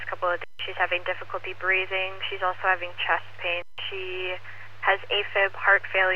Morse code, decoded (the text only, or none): none